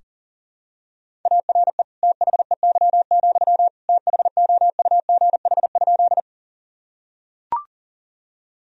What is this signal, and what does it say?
Morse code, heard as a keyed tone, about 40 wpm, 700 Hz. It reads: ARE THEY, THOUGH?
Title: Sentences from Top 1000 Words - Review Part 2 - 40wpm